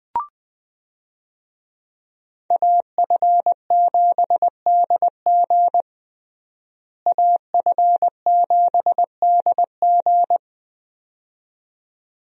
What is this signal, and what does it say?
Morse code, heard as a keyed tone, about 20 wpm, 700 Hz. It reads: AF7DG AF7DG
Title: USA Call Signs - 5 Characters - Repeated 2x - 20wpm